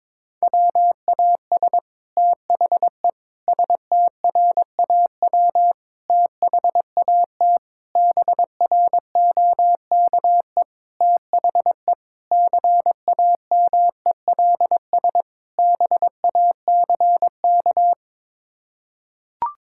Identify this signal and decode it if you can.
Morse code, heard as a keyed tone, about 22 wpm, 700 Hz. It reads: WAS THE STRAW THAT BROKE THE CAMELS BACK
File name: practice / 22wpm / Sets of 8 Words - 22wpm